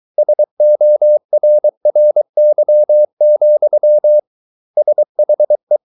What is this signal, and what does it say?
Morse code, heard as a keyed tone, about 23 wpm, 600 Hz. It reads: SORRY, SHE